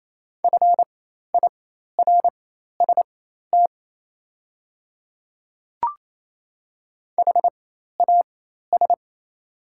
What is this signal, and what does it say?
Morse code, heard as a keyed tone, about 27 wpm, spaced as for 13 wpm, 690 Hz. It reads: FIRST HAS
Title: 4x Character Spacing - Top 200 Words - 28wpm